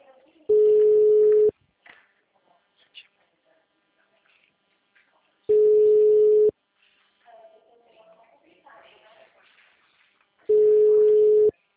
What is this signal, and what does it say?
Ringback tone (Europe, 425 Hz), with speech.
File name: out- -2013- - - .28